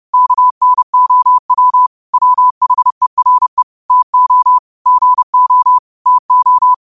Morse code, heard as a keyed tone, about 30 wpm, 1000 Hz. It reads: KNOW WHERE TO GO TO
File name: qotdfiles / 30wpm